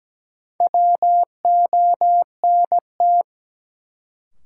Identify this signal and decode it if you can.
Morse code, heard as a keyed tone, about 17 wpm, 700 Hz. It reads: WONT
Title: Sentences from Top 1000 Words - Review Part 6 - New Word Drill - 17wpm